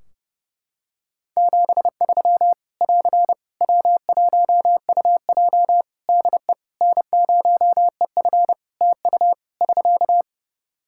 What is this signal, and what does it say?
Morse code, heard as a keyed tone, about 30 wpm, 700 Hz. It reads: 73 <AR> W1UJ DE N0EF TU <SK>
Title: Practice QSO - Exchange 4 - Ending - 30wpm